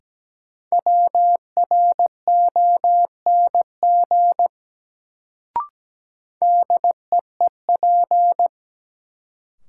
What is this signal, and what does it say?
Morse code, heard as a keyed tone, about 17 wpm, 700 Hz. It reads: WRONG DEEP
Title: Sets of 5 Words Encore - New Word Drill - 17wpm